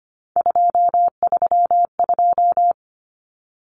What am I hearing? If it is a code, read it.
Morse code, heard as a keyed tone, about 25 wpm, 700 Hz. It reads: 232
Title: Numbers - 3-Digit 25wpm